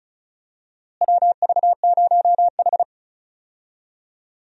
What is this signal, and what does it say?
Morse code, heard as a keyed tone, about 36 wpm, 700 Hz. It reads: WV0H
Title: USA Call Signs - 4 Characters - Repeated 3x - 35wpm